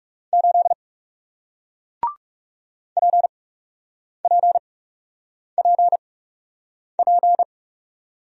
Morse code, heard as a keyed tone, about 41 wpm, spaced as for 8 wpm, 700 Hz. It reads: 8 PPPP